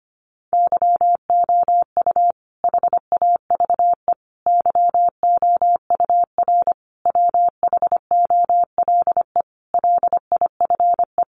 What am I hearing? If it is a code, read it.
Morse code, heard as a keyed tone, about 25 wpm, 700 Hz. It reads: YOU HAVE YOUR WHOLE LIFE
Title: Sentences from Top 400 Words 25wpm